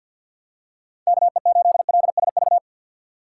Morse code, heard as a keyed tone, about 47 wpm, 700 Hz. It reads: KE8LSV
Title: USA Call Signs - 6 Characters - 4x Character Spacing - 50wpm